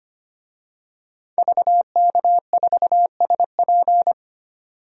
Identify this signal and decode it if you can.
Morse code, heard as a keyed tone, about 25 wpm, 700 Hz. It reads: VK4SP
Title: DXCC-100 Call Signs - 5 Characters - 25wpm